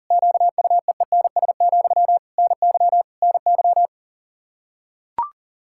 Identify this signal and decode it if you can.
Morse code, heard as a keyed tone, about 40 wpm, 700 Hz. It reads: QUEENS, NY NY
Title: K1USN SST QSO Practice 40wpm